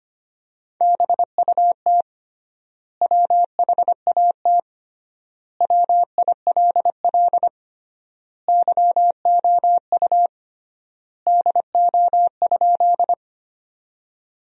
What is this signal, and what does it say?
Morse code, heard as a keyed tone, about 25 wpm, 700 Hz. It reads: BUT WHAT WILL YOU DO?
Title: 3x Word-Spacing - Sentences from Top 100 Words - 25wpm